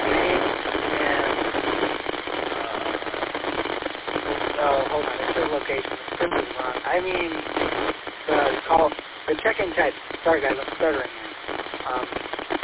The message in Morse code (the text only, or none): none